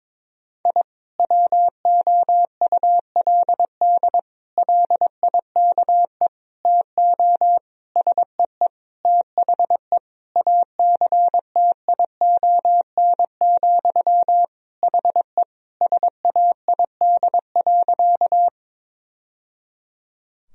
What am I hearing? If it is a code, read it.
Morse code, heard as a keyed tone, about 22 wpm, 700 Hz. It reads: I WOULD LIKE TO SEE THE ACTION, HE SAID.
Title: Sentences from Top 1000 Words - Review Part 4 - 22wpm